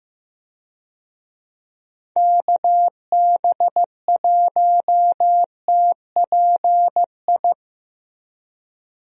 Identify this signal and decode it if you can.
Morse code, heard as a keyed tone, about 15 wpm, 700 Hz. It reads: KB1TPI
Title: USA Call Signs - 6 Characters - Repeated 3x - 15wpm